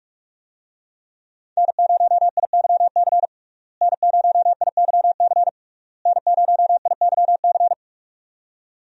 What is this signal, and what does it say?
Morse code, heard as a keyed tone, about 44 wpm, 700 Hz. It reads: N0IYC N0IYC N0IYC
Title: USA Call Signs - 5 Characters - Repeated 3x - 45wpm